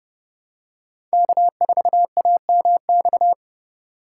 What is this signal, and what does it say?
Morse code, heard as a keyed tone, about 30 wpm, 700 Hz. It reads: K4AMX
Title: USA Call Signs - 5 Characters - 8x Character Spacing - 30wpm